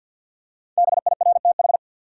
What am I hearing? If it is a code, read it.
Morse code, heard as a keyed tone, about 52 wpm, 700 Hz. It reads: BIRTH
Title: Five Letter Words - 50wpm